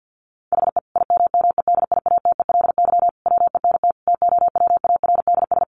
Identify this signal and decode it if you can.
Morse code, heard as a keyed tone, about 50 wpm, 700 Hz. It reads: HE IMMEDIATELY WENT TOWARDS
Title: Sentences from Top 1100 Words 50wpm